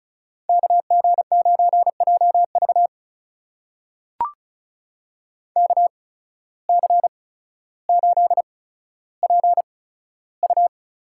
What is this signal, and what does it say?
Morse code, heard as a keyed tone, about 34 wpm, spaced as for 15 wpm, 700 Hz. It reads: KG9JV KC8PU
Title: USA Call Signs - 8x Character Spacing - 35wpm